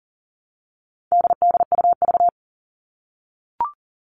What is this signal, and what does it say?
Morse code, heard as a keyed tone, about 40 wpm, 700 Hz. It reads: DDUV
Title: All Cut Numbers - 4-Digit 40wpm